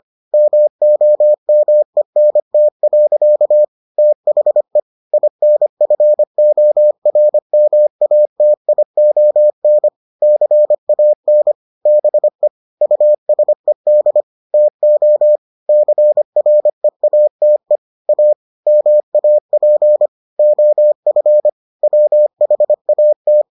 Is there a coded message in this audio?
Yes: Morse code from 0.3 to 23.4 s, heard as a keyed tone, about 25 wpm, 600 Hz. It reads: MOMENT. THE INFORMATION CAN BE USED TO CREATE A MAP OF WHAT